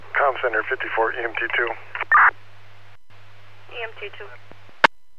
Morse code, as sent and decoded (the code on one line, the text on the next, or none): none